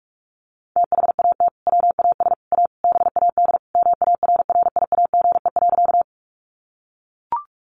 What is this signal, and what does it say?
Morse code, heard as a keyed tone, about 44 wpm, 700 Hz. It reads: THAT WAS A BAD MARRIAGE.
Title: Sentences from Top 800 Words 45wpm